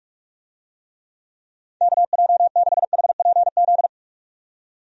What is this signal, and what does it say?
Morse code, heard as a keyed tone, about 46 wpm, 700 Hz. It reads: KJ6HPZ
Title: USA Call Signs - 6 Characters - 45wpm